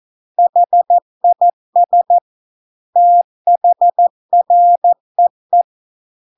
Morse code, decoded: HIS THREE

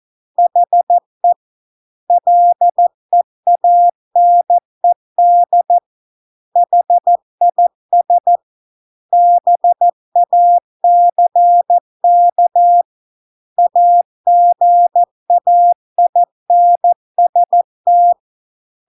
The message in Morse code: HE LEANED HIS BACK AGAINST

